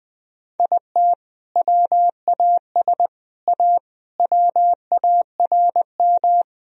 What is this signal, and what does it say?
Morse code, heard as a keyed tone, about 20 wpm, 700 Hz. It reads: IT WAS A WARM
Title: Sentences from Top 1000 Words - Review Part 2 - 20wpm